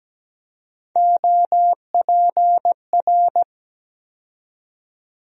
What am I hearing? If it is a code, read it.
Morse code, heard as a keyed tone, about 17 wpm, 700 Hz. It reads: OPR
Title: USA Call Sign Postfixes - 3 Letters 17wpm